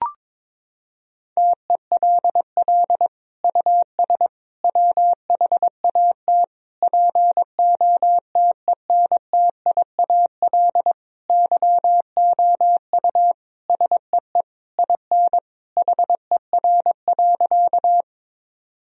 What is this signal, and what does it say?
Morse code, heard as a keyed tone, about 22 wpm, 700 Hz. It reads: TELL US WHAT POTENTIAL YOU SEE IN HER.
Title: Sentences from Top 1000 Words - Review Part 4 - 22wpm